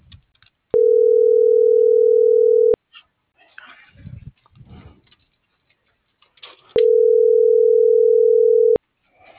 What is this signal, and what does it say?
Ringback tone (North America), 440 + 480 Hz.